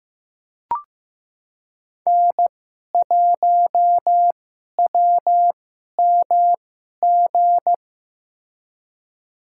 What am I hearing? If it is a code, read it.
Morse code, heard as a keyed tone, about 15 wpm, spaced as for 11 wpm, 700 Hz. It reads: N1WMG
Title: USA Call Signs - 5 Characters - 2x Character Spacing - 15wpm